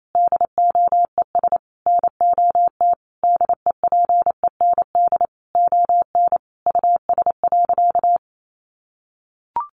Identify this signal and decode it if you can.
Morse code, heard as a keyed tone, about 28 wpm, 700 Hz. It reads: DOES NOT DEPEND ON US.
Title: Sentences from Top 1300 Words 28wpm